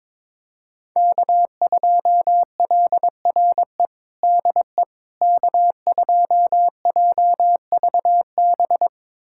Morse code, heard as a keyed tone, about 22 wpm, 700 Hz. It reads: K2LRE DE K2JVB
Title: USA Callsign DE Callsign - 5 Characters - 22wpm